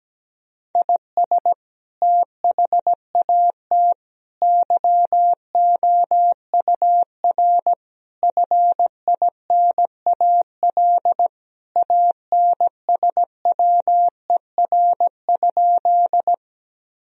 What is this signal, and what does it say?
Morse code, heard as a keyed tone, about 17 wpm, 700 Hz. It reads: IS THAT YOUR FINAL ANSWER?
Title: Sentences from Top 1000 Words - Review Part 2 - 17wpm